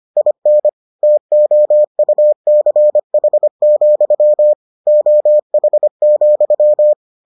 Morse code, heard as a keyed tone, about 25 wpm, 600 Hz. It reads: IN TOUCH, OH,